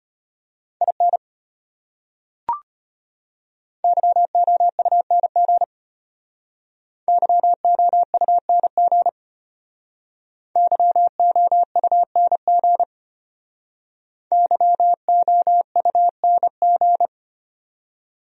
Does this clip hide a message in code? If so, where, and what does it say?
Yes: Morse code from 0.8 to 17.1 s, heard as a keyed tone, about 33 wpm, 700 Hz. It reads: IN YOUNG YOUNG YOUNG YOUNG